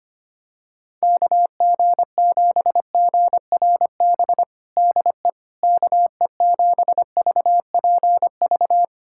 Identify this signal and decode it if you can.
Morse code, heard as a keyed tone, about 25 wpm, 700 Hz. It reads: KG7GRB DE KE7VPV
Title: USA Callsign DE Callsign - 6 Characters - 25wpm